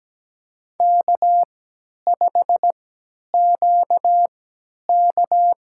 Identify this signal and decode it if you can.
Morse code, heard as a keyed tone, about 17 wpm, spaced as for 10 wpm, 700 Hz. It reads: K5QK
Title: USA Call Signs - 3x Character Spacing - 17wpm